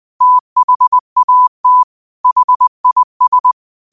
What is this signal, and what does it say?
Morse code, heard as a keyed tone, about 20 wpm, 1000 Hz. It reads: THAT HIS
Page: New quotes every day in morse code at 20 Words per minute.